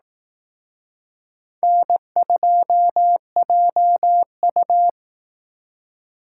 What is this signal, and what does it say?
Morse code, heard as a keyed tone, about 18 wpm, 700 Hz. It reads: N2JU
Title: Speed Racing - USA 1x2 Call Signs 15wpm